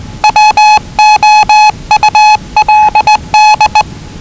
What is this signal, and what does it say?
Morse code, heard as a keyed tone, about 20 wpm, 840 Hz. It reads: WOULD
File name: WellingtonRadioZlwCloseDownOnHfAndMf / 01-Zlw 4277khz 417.5 qrt until 0925z